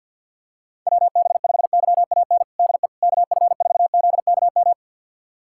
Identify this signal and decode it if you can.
Morse code, heard as a keyed tone, about 53 wpm, 700 Hz. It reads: WB5YAN DE KF4ZKM
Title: USA Callsign DE Callsign 50wpm